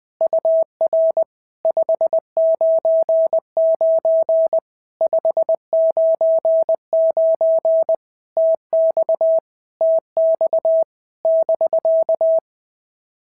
Morse code, heard as a keyed tone, about 20 wpm, 650 Hz. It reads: UR 599 599 TX TX <BK>
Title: POTA QSO Practice - 20wpm